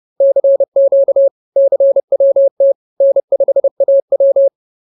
Morse code, heard as a keyed tone, about 30 wpm, 550 Hz. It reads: CQ CWT N5AW
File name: CWT-215-30